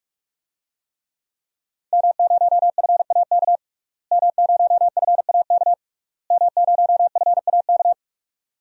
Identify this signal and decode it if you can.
Morse code, heard as a keyed tone, about 41 wpm, 700 Hz. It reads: M0FAK M0FAK M0FAK
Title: DXCC-100 Call Signs - 5 Characters - Repeated 3x - 45wpm